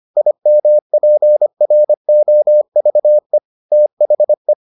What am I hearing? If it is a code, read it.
Morse code, heard as a keyed tone, about 25 wpm, 600 Hz. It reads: IMPROVE THE